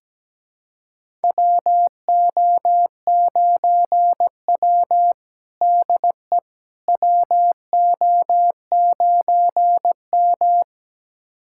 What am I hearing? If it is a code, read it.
Morse code, heard as a keyed tone, about 17 wpm, 700 Hz. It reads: WO9W DE WO9M
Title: USA Callsign DE Callsign - 4 Characters - 17wpm